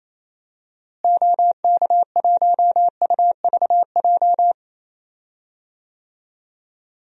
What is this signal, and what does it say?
Morse code, heard as a keyed tone, about 27 wpm, 700 Hz. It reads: OK1UVJ
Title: DXCC-100 Call Signs - 6 Characters - Repeated 2x - 28wpm